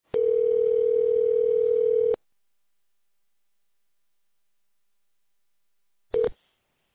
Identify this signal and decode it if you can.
Ringback tone (North America), 440 + 480 Hz.